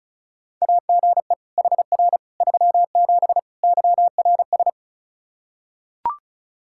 Morse code, heard as a keyed tone, about 36 wpm, 700 Hz. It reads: AGE HR 37 YRS
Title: Practice QSO - Exchange 3 - More Info - 35wpm